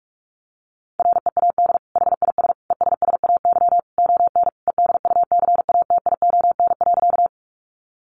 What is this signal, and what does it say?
Morse code, heard as a keyed tone, about 44 wpm, 700 Hz. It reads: READ HIS ESSAY ON EDUCATION.
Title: Sentences from Top 1500 Words 45wpm